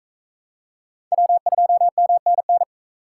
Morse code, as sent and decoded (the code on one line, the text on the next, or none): .-- ..--- -- -. -.
W2MNN